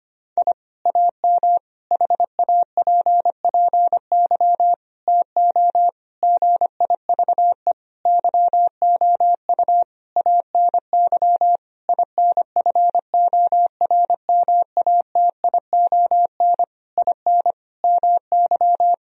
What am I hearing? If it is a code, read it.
Morse code, heard as a keyed tone, about 25 wpm, 700 Hz. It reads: I AM HAPPY TO GIVE YOU ANY INFORMATION IN MY
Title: Sentences from Top 1000 Words - Review Part 3 - 25wpm